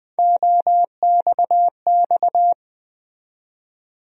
Morse code, heard as a keyed tone, about 20 wpm, 700 Hz. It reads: OXX
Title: USA Call Sign Postfixes - 3 Letters 20wpm